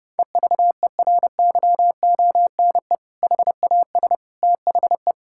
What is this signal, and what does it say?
Morse code, heard as a keyed tone, about 32 wpm, 700 Hz. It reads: EVERYONE HAS THE
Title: Sentences from Top 1000 Words - Review Part 5 - 30wpm